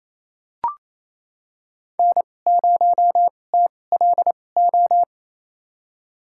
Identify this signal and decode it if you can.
Morse code, heard as a keyed tone, about 28 wpm, spaced as for 20 wpm, 700 Hz. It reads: N0TLO
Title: USA Call Signs - 2x Character Spacing - 28wpm